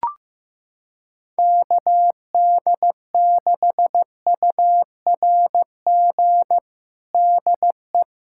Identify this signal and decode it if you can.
Morse code, heard as a keyed tone, about 15 wpm, 700 Hz. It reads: KD6URG DE
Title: USA Callsign DE Callsign 15wpm